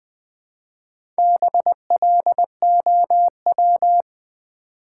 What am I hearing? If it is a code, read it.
Morse code, heard as a keyed tone, about 20 wpm, 700 Hz. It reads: BLOW
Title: Build A Word - 1 Letter At A Time - 4-Letter Words - 20wpm